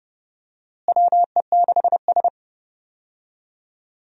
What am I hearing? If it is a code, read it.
Morse code, heard as a keyed tone, about 30 wpm, 700 Hz. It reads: WE6S
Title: USA Call Signs - 4 Characters - Repeated 3x - 30wpm